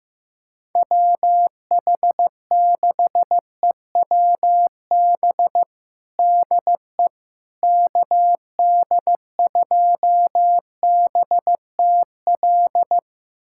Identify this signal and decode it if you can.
Morse code, heard as a keyed tone, about 16 wpm, 700 Hz. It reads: WH6EWB DE KD2BTL
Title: USA Callsign DE Callsign - 6 Characters - 15wpm